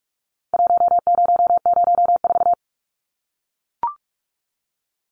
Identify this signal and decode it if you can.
Morse code, heard as a keyed tone, about 44 wpm, 700 Hz. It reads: 1004
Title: Numbers - 4-Digit 45wpm